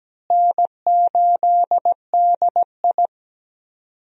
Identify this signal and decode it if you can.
Morse code, heard as a keyed tone, about 17 wpm, 700 Hz. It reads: N8DI
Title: USA Call Signs - 4 Characters - Repeated 3x - 17wpm